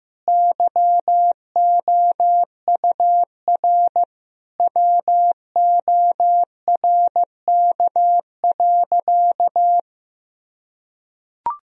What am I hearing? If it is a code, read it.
Morse code, heard as a keyed tone, about 15 wpm, 700 Hz. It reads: YOUR WORK.
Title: Sentences from Top 1000 Words - Review Part 4 - 15wpm